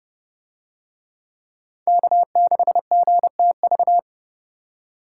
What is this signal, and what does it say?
Morse code, heard as a keyed tone, about 26 wpm, 700 Hz. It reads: K6GTV